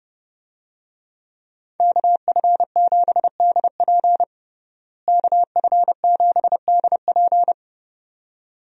Morse code, heard as a keyed tone, about 31 wpm, 700 Hz. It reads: KF7DP KF7DP
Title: USA Call Signs - 5 Characters - Repeated 2x - 30wpm